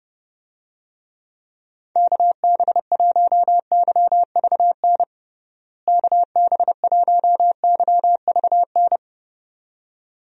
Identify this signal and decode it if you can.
Morse code, heard as a keyed tone, about 32 wpm, 700 Hz. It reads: KB1YVN KB1YVN
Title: USA Call Signs - 6 Characters - Repeated 2x - 30wpm